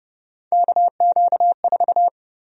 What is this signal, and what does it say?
Morse code, heard as a keyed tone, about 32 wpm, 700 Hz. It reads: KQ4
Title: USA Call Sign Prefixes - 30wpm